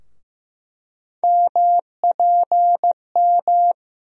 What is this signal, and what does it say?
Morse code, heard as a keyed tone, about 15 wpm, 700 Hz. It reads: MPM
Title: USA Call Sign Postfixes - 3 Letters 15wpm